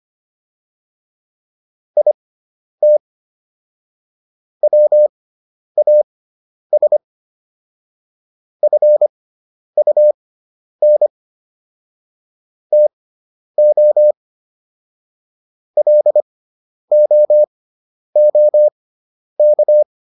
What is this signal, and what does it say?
Morse code, heard as a keyed tone, about 25 wpm, spaced as for 10 wpm, 600 Hz. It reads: IT WAS FUN TO LOOK